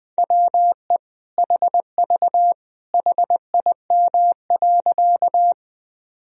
Morse code, heard as a keyed tone, about 20 wpm, 700 Hz. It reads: WE HV HIM.
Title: Sentences from Top 100 Words and Abbreviations 20wpm